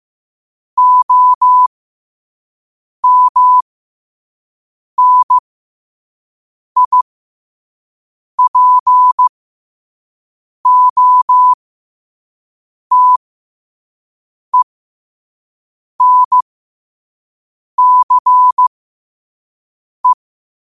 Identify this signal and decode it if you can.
Morse code, heard as a keyed tone, about 15 wpm, spaced as for 5 wpm, 1000 Hz. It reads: OMNIPOTENCE
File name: qotdfiles / 5wpm